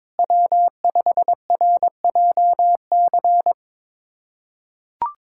Morse code, heard as a keyed tone, about 22 wpm, 700 Hz. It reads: W5RJC
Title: USA Callsign DE Callsign - 5 Characters - 22wpm